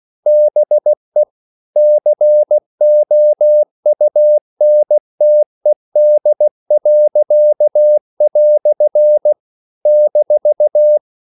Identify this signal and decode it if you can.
Morse code, heard as a keyed tone, about 16 wpm, 600 Hz. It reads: BE COUNTED." -
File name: CW / Practice / EnglishQuotations0116